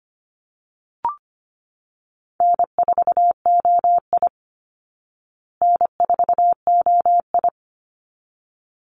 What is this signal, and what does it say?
Morse code, heard as a keyed tone, about 25 wpm, 700 Hz. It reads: N4OI N4OI
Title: Easy Call Signs Repeated 2x - 25wpm